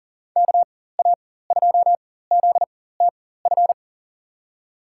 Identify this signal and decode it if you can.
Morse code, heard as a keyed tone, about 40 wpm, spaced as for 19 wpm, 700 Hz. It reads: KA2ZTF
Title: USA Call Signs - 6 Characters - 4x Character Spacing - 40wpm